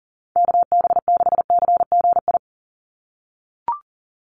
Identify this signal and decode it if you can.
Morse code, heard as a keyed tone, about 40 wpm, 700 Hz. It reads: KB6CGI
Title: USA Call Signs 40wpm